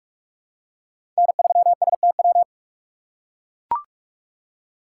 Morse code, heard as a keyed tone, about 44 wpm, 700 Hz. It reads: N3STW
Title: USA Call Signs - 5 Characters - 6x Character Spacing - 45wpm